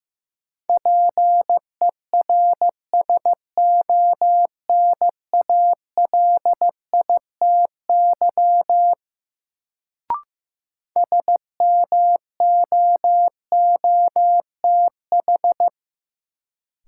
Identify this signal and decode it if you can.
Morse code, heard as a keyed tone, about 16 wpm, 700 Hz. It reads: PERSONALITY SMOOTH
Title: Sentences from Top 1400 Words - New Word Drill - 15wpm